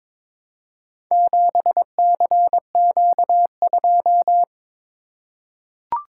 Morse code, heard as a keyed tone, about 22 wpm, 700 Hz. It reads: 7CQ2
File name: practice / 22wpm / Four Letter-Number Combinations - 22wpm